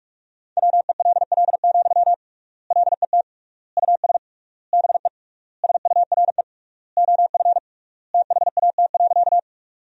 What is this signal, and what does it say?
Morse code, heard as a keyed tone, about 46 wpm, 700 Hz. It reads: WELL, LET US BE SURE OF THAT.
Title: 3x Word-Spacing - Sentences from Top 300 Words - 45wpm